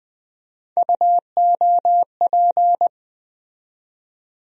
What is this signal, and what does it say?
Morse code, heard as a keyed tone, about 20 wpm, 700 Hz. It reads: UOP
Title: USA Call Sign Postfixes - 3 Letters 20wpm